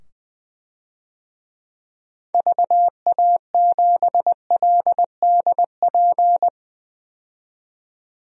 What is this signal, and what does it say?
Morse code, heard as a keyed tone, about 21 wpm, 700 Hz. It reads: VA7LDP
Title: DXCC-100 Call Signs - 6 Characters - 20wpm